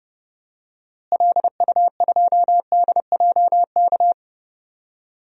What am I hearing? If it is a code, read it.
Morse code, heard as a keyed tone, about 30 wpm, 700 Hz. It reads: LU2DJK
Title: DXCC-100 Call Signs - 6 Characters - Repeated 3x - 30wpm